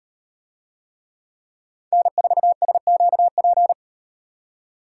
Morse code, heard as a keyed tone, about 35 wpm, 700 Hz. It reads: N4SQP